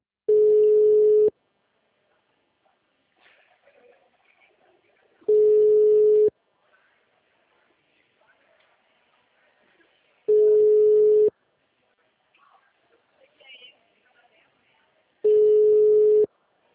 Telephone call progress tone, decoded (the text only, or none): ringback tone